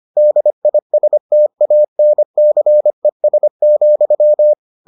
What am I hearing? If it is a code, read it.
Morse code, heard as a keyed tone, about 25 wpm, 600 Hz. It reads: DISTANCES,